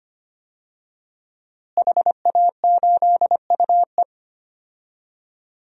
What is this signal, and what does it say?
Morse code, heard as a keyed tone, about 25 wpm, 700 Hz. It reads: HA8UE
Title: DXCC-100 Call Signs - 5 Characters - Repeated 2x - 25wpm